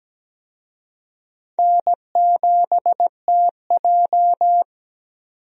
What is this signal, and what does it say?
Morse code, heard as a keyed tone, about 17 wpm, 700 Hz. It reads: N7TJ
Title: USA Call Signs - 4 Characters - 17wpm